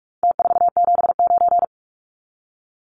Morse code, heard as a keyed tone, about 44 wpm, 700 Hz. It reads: T479
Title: Numbers - Contest Style - 4-Digit 45wpm